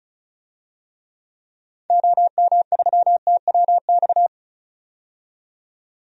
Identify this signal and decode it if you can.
Morse code, heard as a keyed tone, about 34 wpm, 700 Hz. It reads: OM3TWX
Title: DXCC-100 Call Signs - 6 Characters - 35wpm